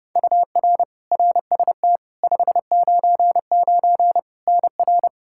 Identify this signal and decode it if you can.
Morse code, heard as a keyed tone, about 30 wpm, 700 Hz. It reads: UR RST 599 NR
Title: Practice QSO - Exchange 1 - Beginning - 30wpm